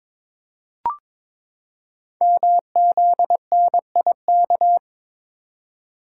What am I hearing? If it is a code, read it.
Morse code, heard as a keyed tone, about 22 wpm, 700 Hz. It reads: MZNIK
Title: Code Groups - 5 Letters - 22wpm